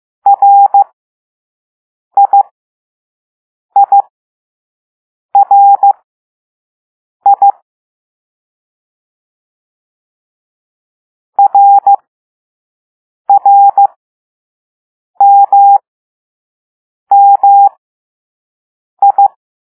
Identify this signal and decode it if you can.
Morse code, heard as a keyed tone, about 15 wpm, spaced as for 5 wpm, 800 Hz. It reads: RIIRI RRMMI
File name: Morse / Level2B